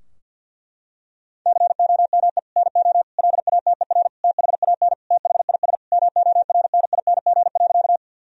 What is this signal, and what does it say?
Morse code, heard as a keyed tone, about 50 wpm, 700 Hz. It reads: COME NO LATER THAN THIS MORNING.
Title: Sentences from Top 1000 Words - Review Part 4 - 50wpm